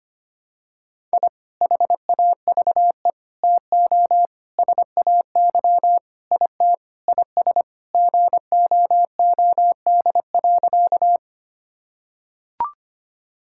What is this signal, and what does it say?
Morse code, heard as a keyed tone, about 25 wpm, 700 Hz. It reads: I HAVE TO SAY IT IS GOOD.
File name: practice / 25wpm / Sentences with 4-Word Phrases - 25wpm